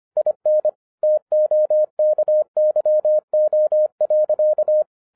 Morse code, heard as a keyed tone, about 25 wpm, 610 Hz. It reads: IN TOKYO.